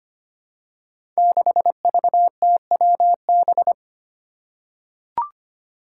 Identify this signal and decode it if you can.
Morse code, heard as a keyed tone, about 25 wpm, 700 Hz. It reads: 6VTWB
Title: Five Challenging Letter-Number Combinations 25wpm